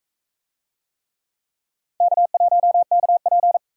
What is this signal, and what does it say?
Morse code, heard as a keyed tone, about 41 wpm, 700 Hz. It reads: K1KP